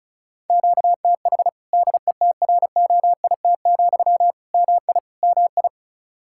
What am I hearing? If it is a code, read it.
Morse code, heard as a keyed tone, about 36 wpm, 700 Hz. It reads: QTH DETROIT, MI MI
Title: Practice QSO - Exchange 1 - Beginning - 35wpm